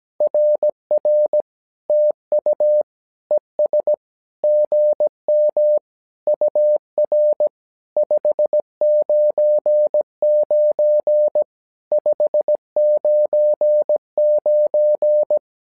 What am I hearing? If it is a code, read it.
Morse code, heard as a keyed tone, about 17 wpm, 600 Hz. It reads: RR TU ES GM UR 599 599
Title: SOTA QSO Practice 17wpm